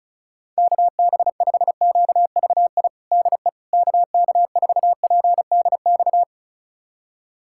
Morse code, heard as a keyed tone, about 34 wpm, 700 Hz. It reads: KB5QVI DE KK4PDX
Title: USA Callsign DE Callsign - 6 Characters - 35wpm